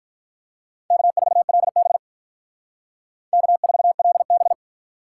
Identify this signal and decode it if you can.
Morse code, heard as a keyed tone, about 45 wpm, 700 Hz. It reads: K4LB K4LB